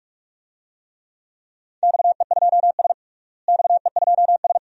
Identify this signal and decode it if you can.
Morse code, heard as a keyed tone, about 47 wpm, 700 Hz. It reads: XE2S XE2S
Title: DXCC-100 Call Signs - 4 Characters - Repeated 2x - 45wpm